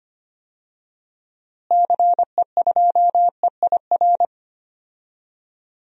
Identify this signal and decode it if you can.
Morse code, heard as a keyed tone, about 25 wpm, 700 Hz. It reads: CE2EIR
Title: DXCC-100 Call Signs - 6 Characters - Repeated 3x - 25wpm